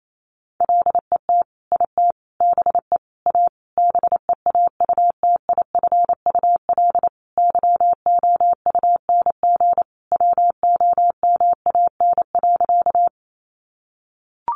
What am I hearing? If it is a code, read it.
Morse code, heard as a keyed tone, about 28 wpm, 700 Hz. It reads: LET IT BE A BEAUTIFUL YOUNG WOMAN.
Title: Sentences from Top 1100 Words 28wpm